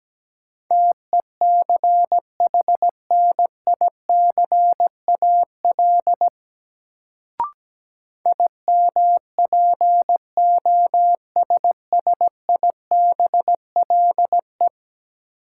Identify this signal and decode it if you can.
Morse code, heard as a keyed tone, about 17 wpm, 700 Hz. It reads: TECHNICAL IMPOSSIBLE
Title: Sentences from Top 1100 Words - New Word Drill - 17wpm